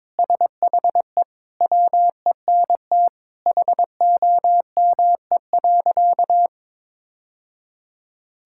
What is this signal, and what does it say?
Morse code, heard as a keyed tone, about 22 wpm, 700 Hz. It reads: SHE WENT HOME.
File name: practice / 22wpm / Sentences with 2-Word Phrases - 22wpm